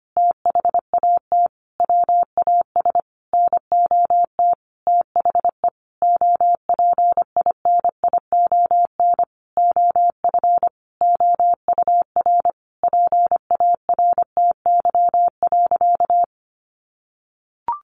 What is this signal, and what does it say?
Morse code, heard as a keyed tone, about 25 wpm, 700 Hz. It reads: THAT WAS NOT THE OPINION OF OUR PARTY.
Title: Sentences from Top 1400 Words 25wpm